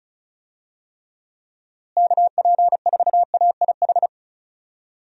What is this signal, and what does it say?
Morse code, heard as a keyed tone, about 36 wpm, 700 Hz. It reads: KP4AIH
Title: DXCC-100 Call Signs - 6 Characters - Repeated 2x - 35wpm